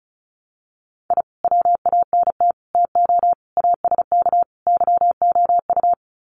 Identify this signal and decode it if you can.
Morse code, heard as a keyed tone, about 34 wpm, 700 Hz. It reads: I WANT TO ASK YOU
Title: Sets of 5 Words 35wpm